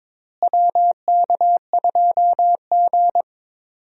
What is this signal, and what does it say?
Morse code, heard as a keyed tone, about 21 wpm, 700 Hz. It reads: WK2G
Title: USA Call Signs - 4 Characters - Repeated 3x - 22wpm